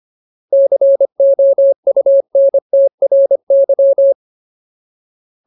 Morse code, heard as a keyed tone, about 25 wpm, 550 Hz. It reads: COUNTRY
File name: Ultimate Word List / 25WPM Ultimate